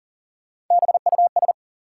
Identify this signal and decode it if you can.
Morse code, heard as a keyed tone, about 40 wpm, 700 Hz. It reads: BUS
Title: Sentences from Top 1000 Words - Review Part 6 - New Word Drill - 40wpm